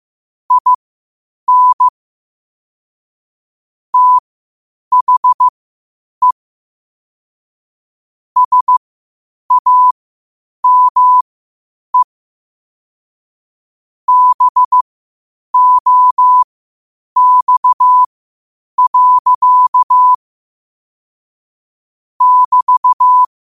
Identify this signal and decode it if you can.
Morse code, heard as a keyed tone, about 15 wpm, spaced as for 8 wpm, 1000 Hz. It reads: IN THE SAME BOX. <BT>
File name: qotdfiles / 8wpm